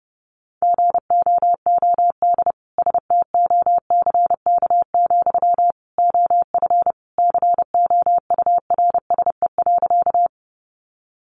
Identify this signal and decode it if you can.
Morse code, heard as a keyed tone, about 30 wpm, 700 Hz. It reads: GOOD STOCK, OF COURSE.
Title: Sentences from Top 800 Words 30wpm